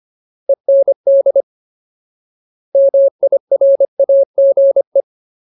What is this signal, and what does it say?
Morse code, heard as a keyed tone, about 25 wpm, 550 Hz. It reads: END MIRAGE
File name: Ultimate Word List / 25WPM Ultimate